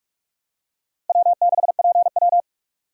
Morse code, heard as a keyed tone, about 46 wpm, 700 Hz. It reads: W6PW
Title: USA Call Signs - 4 Characters - Repeated 2x - 45wpm